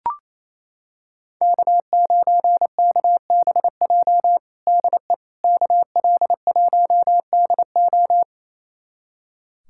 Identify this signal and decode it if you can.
Morse code, heard as a keyed tone, about 28 wpm, 700 Hz. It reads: K9KBJ DE KL1DO
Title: USA Callsign DE Callsign - 5 Characters - 28wpm